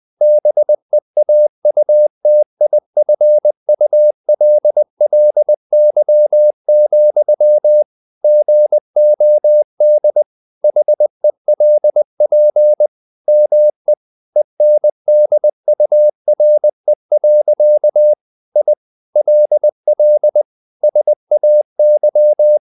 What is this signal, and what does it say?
Morse code, heard as a keyed tone, about 20 wpm, 600 Hz. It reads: BEAUTIFULLY, GOD HELP ME ENDURE. I LL SAY
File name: the old man and the sea part 2 20wpm